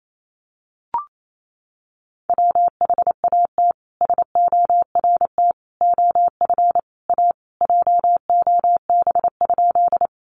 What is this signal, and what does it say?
Morse code, heard as a keyed tone, about 28 wpm, 700 Hz. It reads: WHAT SORT OF A JOB?
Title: Sentences from Top 1300 Words 28wpm